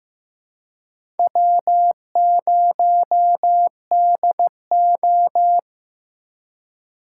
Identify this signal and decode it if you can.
Morse code, heard as a keyed tone, about 15 wpm, 700 Hz. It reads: W0DO
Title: USA Call Signs - 4 Characters - 3x Character Spacing - 15wpm